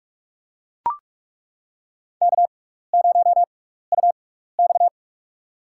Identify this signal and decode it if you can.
Morse code, heard as a keyed tone, about 42 wpm, spaced as for 15 wpm, 700 Hz. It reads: K0UX
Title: USA Call Signs - 6x Character Spacing - 45wpm